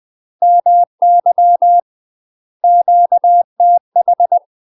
Morse code, heard as a keyed tone, about 20 wpm, 700 Hz. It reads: MY QTH
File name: UK MASTER QSOs TRAINING / 20WPM / UK MASTER QSOs 0053